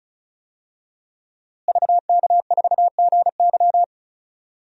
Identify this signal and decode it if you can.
Morse code, heard as a keyed tone, about 34 wpm, 700 Hz. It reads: VK4GY
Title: DXCC-100 Call Signs - 5 Characters - Repeated 2x - 35wpm